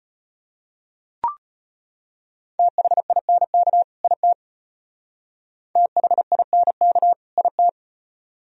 Morse code, heard as a keyed tone, about 32 wpm, 700 Hz. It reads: THINK IT THINK IT